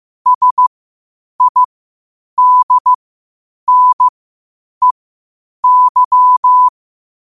Morse code, heard as a keyed tone, about 15 wpm, spaced as for 8 wpm, 1000 Hz. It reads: SIDNEY
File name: qotdfiles / 8wpm